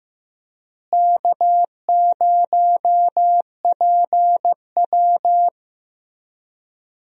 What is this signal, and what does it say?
Morse code, heard as a keyed tone, about 15 wpm, 700 Hz. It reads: K0PW
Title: USA Call Signs - 4 Characters - Repeated 3x - 15wpm